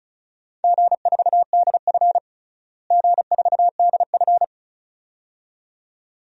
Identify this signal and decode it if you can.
Morse code, heard as a keyed tone, about 34 wpm, 700 Hz. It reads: G4DF G4DF
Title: DXCC-100 Call Signs - 4 Characters - Repeated 2x - 35wpm